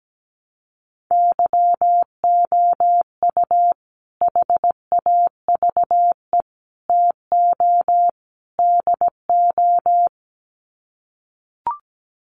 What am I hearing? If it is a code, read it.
Morse code, heard as a keyed tone, about 17 wpm, 700 Hz. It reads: YOU HAVE TO DO
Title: Sets of 4 Words 17wpm